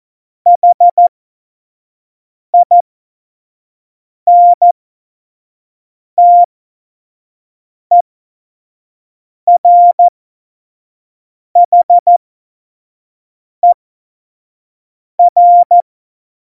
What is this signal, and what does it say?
Morse code, heard as a keyed tone, about 14 wpm, spaced as for 5 wpm, 700 Hz. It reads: HINTERHER
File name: Morsekurs / Witze / Witz 12. Februar 2019, 25 F70